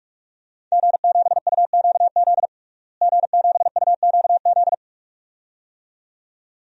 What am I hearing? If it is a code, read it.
Morse code, heard as a keyed tone, about 42 wpm, 700 Hz. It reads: G7UQZ G7UQZ
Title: DXCC-100 Call Signs - 5 Characters - Repeated 2x - 45wpm